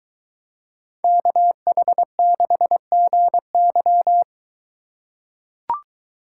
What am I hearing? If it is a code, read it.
Morse code, heard as a keyed tone, about 21 wpm, 700 Hz. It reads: KH6GY